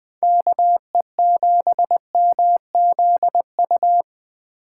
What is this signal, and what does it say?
Morse code, heard as a keyed tone, about 20 wpm, 700 Hz. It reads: KE7MZU
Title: Speed Racing - USA 2x3 Call Signs 17wpm